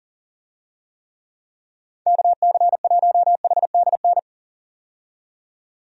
Morse code, heard as a keyed tone, about 38 wpm, 700 Hz. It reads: KC1HDN
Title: USA Call Signs - 6 Characters - 40wpm